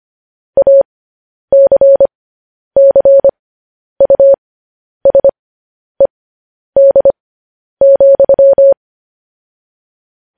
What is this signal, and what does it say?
Morse code, heard as a keyed tone, about 25 wpm, spaced as for 10 wpm, 550 Hz. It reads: ACCUSED,
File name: podcast / News headlines-1x-25x10